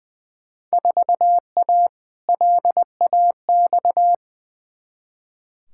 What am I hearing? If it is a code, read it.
Morse code, heard as a keyed tone, about 20 wpm, 700 Hz. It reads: 4A LAX
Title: Field Day Exchanges - 20wpm